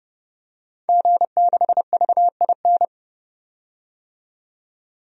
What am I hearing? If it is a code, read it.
Morse code, heard as a keyed tone, about 31 wpm, 700 Hz. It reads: G6VIN
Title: DXCC-100 Call Signs - 5 Characters - 30wpm